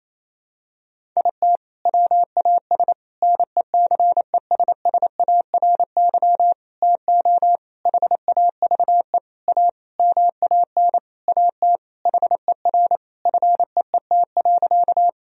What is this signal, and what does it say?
Morse code, heard as a keyed tone, about 28 wpm, 700 Hz. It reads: IT WAS NECESSARY TO HAVE A MAN AT HER FEET.
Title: Sentences from Top 1000 Words - Review Part 1 - 28wpm